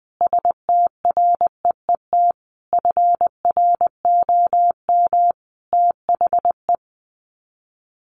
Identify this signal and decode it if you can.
Morse code, heard as a keyed tone, about 20 wpm, 700 Hz. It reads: STREET FROM THE
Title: Sets of 5 Words Encore 20wpm